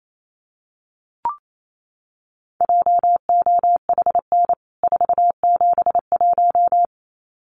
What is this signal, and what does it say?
Morse code, heard as a keyed tone, about 28 wpm, 700 Hz. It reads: JOHN 471
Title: CWT Exchange 28wpm